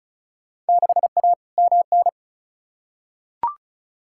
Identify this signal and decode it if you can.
Morse code, heard as a keyed tone, about 36 wpm, 690 Hz. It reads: 6A MN
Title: Field Day Exchanges Encore - 35wpm